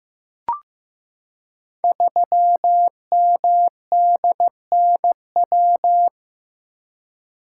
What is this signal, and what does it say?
Morse code, heard as a keyed tone, about 15 wpm, 700 Hz. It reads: 3MDNW
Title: Code Groups - 5 Letter:Numbers - 15wpm